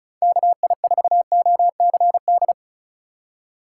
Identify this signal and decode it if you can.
Morse code, heard as a keyed tone, about 34 wpm, 700 Hz. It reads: KI4OCD
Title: USA Call Signs - 6 Characters - 4x Character Spacing - 35wpm